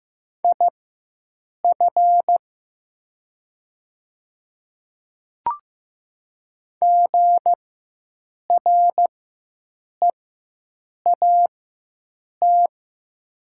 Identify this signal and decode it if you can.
Morse code, heard as a keyed tone, about 15 wpm, spaced as for 7 wpm, 700 Hz. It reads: IF GREAT
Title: 4x Character Spacing - Top 300 Words - 15wpm